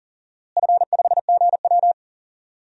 Morse code, heard as a keyed tone, about 40 wpm, 700 Hz. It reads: F5GW
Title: DXCC-100 Call Signs - 4 Characters - Repeated 3x - 40wpm